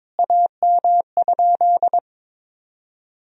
Morse code, heard as a keyed tone, about 21 wpm, 700 Hz. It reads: AM?
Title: Sentences from Top 1000 Words - Review Part 3 - 22wpm